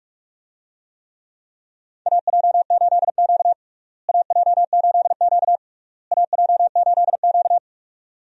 Morse code, heard as a keyed tone, about 46 wpm, 700 Hz. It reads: AJ8Q AJ8Q AJ8Q
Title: USA Call Signs - 4 Characters - Repeated 3x - 45wpm